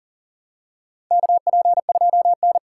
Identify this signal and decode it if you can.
Morse code, heard as a keyed tone, about 38 wpm, 700 Hz. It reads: KP2N
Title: K1USN SST QSO Practice - 40wpm